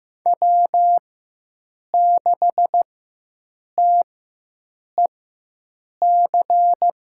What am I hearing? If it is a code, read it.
Morse code, heard as a keyed tone, about 15 wpm, spaced as for 7 wpm, 700 Hz. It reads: W6TEC
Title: USA Call Signs - 5 Characters - 4x Character Spacing - 15wpm